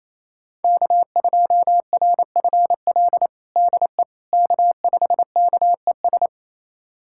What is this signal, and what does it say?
Morse code, heard as a keyed tone, about 27 wpm, 700 Hz. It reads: K2RFL DE K5KES
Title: USA Callsign DE Callsign - 5 Characters - 28wpm